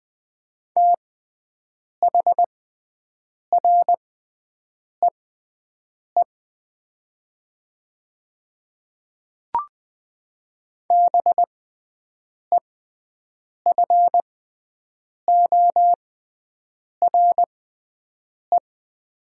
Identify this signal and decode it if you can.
Morse code, heard as a keyed tone, about 20 wpm, spaced as for 7 wpm, 700 Hz. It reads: THREE BEFORE
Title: 6x Character Spacing - Top 200 Words - 20wpm